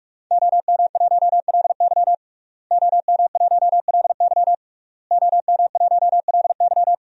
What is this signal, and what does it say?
Morse code, heard as a keyed tone, about 46 wpm, 700 Hz. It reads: OM1LY OM1LY OM1LY
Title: DXCC-100 Call Signs - 5 Characters - Repeated 3x - 45wpm